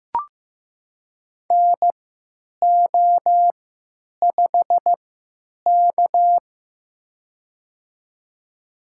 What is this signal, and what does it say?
Morse code, heard as a keyed tone, about 15 wpm, spaced as for 9 wpm, 700 Hz. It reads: NO5K
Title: USA Call Signs - 4 Characters - 3x Character Spacing - 15wpm